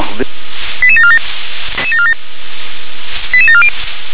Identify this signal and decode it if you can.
Five-tone selective call (ZVEI), heard as quick runs of five tones, tones of about 70 ms, with speech. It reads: 88747 88740